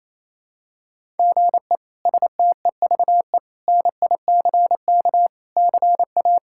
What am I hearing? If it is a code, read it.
Morse code, heard as a keyed tone, about 28 wpm, 700 Hz. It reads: GE STEVE NICK CA
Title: K1USN SST QSO Practice - 28wpm